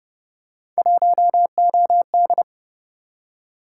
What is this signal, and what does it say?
Morse code, heard as a keyed tone, about 31 wpm, 700 Hz. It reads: 1OD